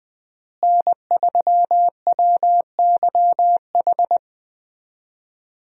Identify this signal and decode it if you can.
Morse code, heard as a keyed tone, about 20 wpm, 700 Hz. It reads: N3WYH
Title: USA Call Signs - 5 Characters - Repeated 3x - 20wpm